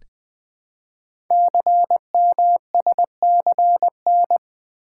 Morse code, heard as a keyed tone, about 20 wpm, 700 Hz. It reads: CMSCN
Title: Code Groups - 5 Letters - 20wpm